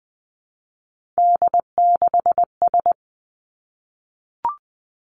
Morse code, heard as a keyed tone, about 20 wpm, 700 Hz. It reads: D6S
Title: Three Challenging Letter-Number Combinations 20wpm